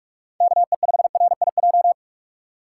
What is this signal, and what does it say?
Morse code, heard as a keyed tone, about 46 wpm, 700 Hz. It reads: KE5RIJ
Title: USA Call Signs - 6 Characters - 2x Character Spacing - 45wpm